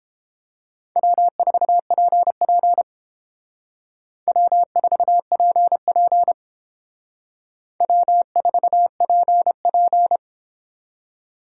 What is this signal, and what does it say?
Morse code, heard as a keyed tone, about 31 wpm, 700 Hz. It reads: W4PP W4PP W4PP